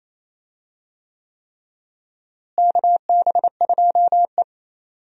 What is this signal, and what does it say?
Morse code, heard as a keyed tone, about 28 wpm, 700 Hz. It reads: KB2E
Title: USA Call Signs - 4 Characters - Repeated 3x - 28wpm